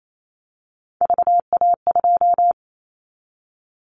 Morse code, heard as a keyed tone, about 28 wpm, 700 Hz. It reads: VA2
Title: Callsign Prefixes for Top 100 DXCC - 28wpm